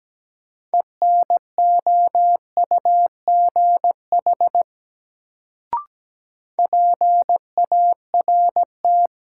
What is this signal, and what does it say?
Morse code, heard as a keyed tone, about 17 wpm, 700 Hz. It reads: ENOUGH PART
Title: Sets of 2 Words - New Word Drill - 17wpm